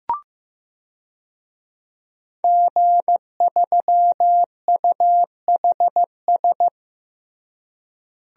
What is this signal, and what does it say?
Morse code, heard as a keyed tone, about 16 wpm, 700 Hz. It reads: G3UHS
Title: DXCC-100 Call Signs - 5 Characters - Repeated 2x - 15wpm